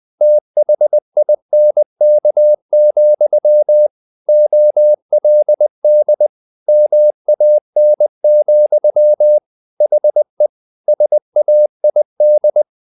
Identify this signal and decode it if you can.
Morse code, heard as a keyed tone, about 20 wpm, 600 Hz. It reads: THINK, OLD MAN, HE SAID
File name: the old man and the sea part 2 20wpm